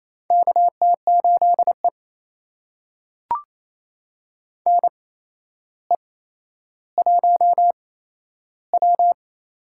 Morse code, heard as a keyed tone, about 28 wpm, spaced as for 12 wpm, 700 Hz. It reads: KT8E NE1W
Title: USA Call Signs - 4 Characters - 8x Character Spacing - 28wpm